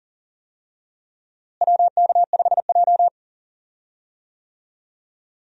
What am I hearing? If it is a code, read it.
Morse code, heard as a keyed tone, about 40 wpm, 700 Hz. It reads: WK5J
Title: USA Call Signs - 4 Characters - Repeated 2x - 40wpm